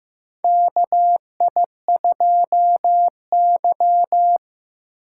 Morse code, heard as a keyed tone, about 15 wpm, 700 Hz. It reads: KI2Y
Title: USA Call Signs - 4 Characters - 2x Character Spacing - 15wpm